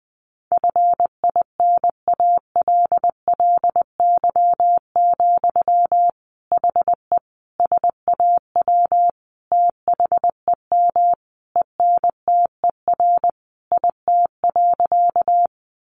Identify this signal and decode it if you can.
Morse code, heard as a keyed tone, about 20 wpm, 700 Hz. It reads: FINALLY, HE SAW THEM ENTER IT.
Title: Sentences from Top 1300 Words 20wpm